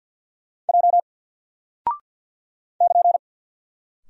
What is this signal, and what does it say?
Morse code, heard as a keyed tone, about 50 wpm, 700 Hz. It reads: J <KN>
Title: Single Character (Letter:Number:Punctuation:Prosign) 50wpm